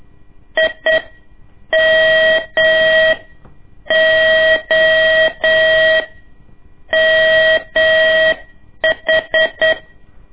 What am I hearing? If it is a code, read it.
Morse code, heard as a keyed tone, about 8 wpm, spaced as for 7 wpm, 640 Hz. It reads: IMOMH
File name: ik1weg2 / CORSI CW / lez2-1